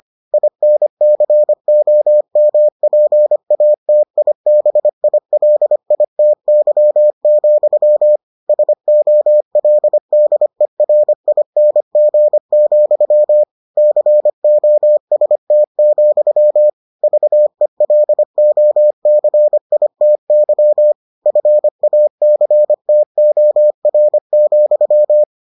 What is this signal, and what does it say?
Morse code, heard as a keyed tone, about 25 wpm, 600 Hz. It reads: INCOMPATIBILITY, SOLDERING, COST, VELOCITY FACTOR,